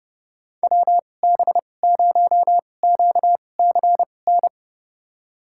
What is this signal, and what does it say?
Morse code, heard as a keyed tone, about 30 wpm, spaced as for 22 wpm, 700 Hz. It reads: WB0QCN
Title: USA Call Signs - 6 Characters - 2x Character Spacing - 30wpm